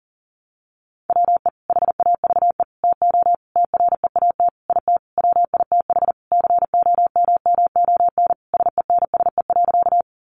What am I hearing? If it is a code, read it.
Morse code, heard as a keyed tone, about 37 wpm, 700 Hz. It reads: WE HAVE TO TREAT IT WITH COMMON SENSE.
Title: Sentences from Top 1300 Words 40wpm